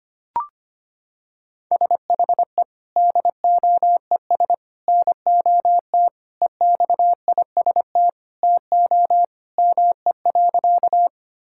Morse code, heard as a keyed tone, about 25 wpm, 700 Hz. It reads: SHE DOES NOT EXIST TO ME.
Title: Sentences from Top 1000 Words - Review Part 5 - 25wpm